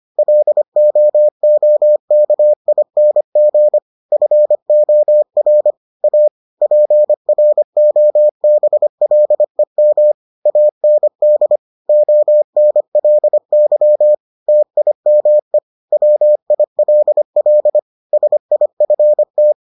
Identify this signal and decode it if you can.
Morse code, heard as a keyed tone, about 25 wpm, 600 Hz. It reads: LOOKING FOR A PROBLEM AND ONLY TIME WILL SIFT